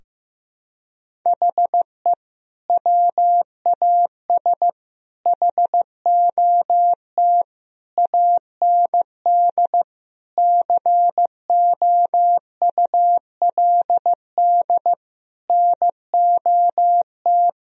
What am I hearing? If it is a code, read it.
Morse code, heard as a keyed tone, about 14 wpm, 700 Hz. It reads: HE WAS HOT AND COULD NOT
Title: Sentences from Top 1000 Words - Review Part 3 - 15wpm